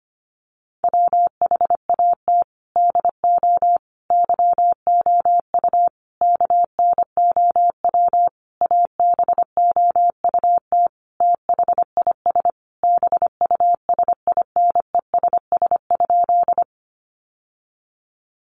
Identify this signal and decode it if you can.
Morse code, heard as a keyed tone, about 25 wpm, 700 Hz. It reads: WHAT DO YOU KNOW ABOUT THIS BUSINESS?
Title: Sentences from Top 600 Words 25wpm